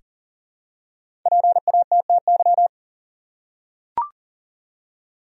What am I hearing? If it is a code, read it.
Morse code, heard as a keyed tone, about 40 wpm, 700 Hz. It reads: PATTY
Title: USA Common Names - Max 5 Letters - 40wpm